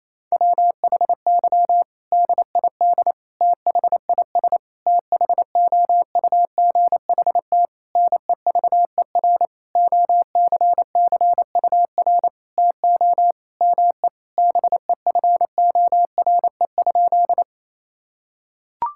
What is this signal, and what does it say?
Morse code, heard as a keyed tone, about 27 wpm, 700 Hz. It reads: WHY DID THIS THOUGHT NEVER OCCUR TO ME BEFORE?
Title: Sentences from Top 1000 Words - Review Part 6 - 28wpm